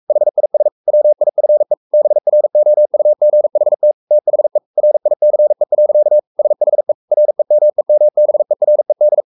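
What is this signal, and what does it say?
Morse code, heard as a keyed tone, about 44 wpm, 600 Hz. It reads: HIS WIFE BROUGHT THE RICE. SHE REMEMBERED
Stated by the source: Morse code at 43 wpm sending Ken Follett's novel Never in English, from CW med Gnister.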